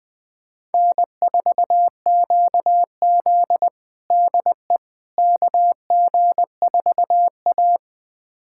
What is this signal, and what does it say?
Morse code, heard as a keyed tone, about 21 wpm, 700 Hz. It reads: N4QZ DE KG4A
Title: USA Callsign DE Callsign - 4 Characters - 20wpm